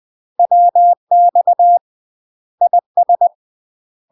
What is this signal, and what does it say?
Morse code, heard as a keyed tone, about 20 wpm, 700 Hz. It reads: WX IS
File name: UK MASTER QSOs TRAINING / 20WPM / UK MASTER QSOs 0064